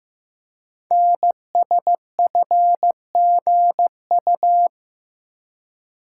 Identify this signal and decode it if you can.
Morse code, heard as a keyed tone, about 15 wpm, 700 Hz. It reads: NSFGU
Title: Code Groups - 5 Letters - 15wpm